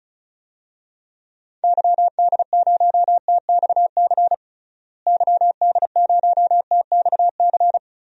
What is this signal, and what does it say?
Morse code, heard as a keyed tone, about 34 wpm, 700 Hz. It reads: YD0TXC YD0TXC
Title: DXCC-100 Call Signs - 6 Characters - Repeated 3x - 35wpm